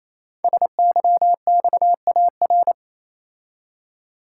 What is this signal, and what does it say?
Morse code, heard as a keyed tone, about 28 wpm, 700 Hz. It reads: SYXAR
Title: Code Groups - 5 Letters - 28wpm